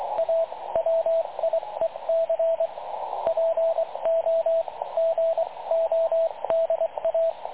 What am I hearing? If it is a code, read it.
Morse code, heard as a keyed tone, about 23 wpm, spaced as for 19 wpm, 660 Hz, with radio noise. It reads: A WIEC POGODA
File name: dlazaawansowanych / nagraniazpasma / websdr recording 3567.0kHz